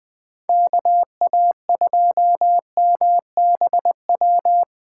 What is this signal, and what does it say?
Morse code, heard as a keyed tone, about 21 wpm, 700 Hz. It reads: KA2MBW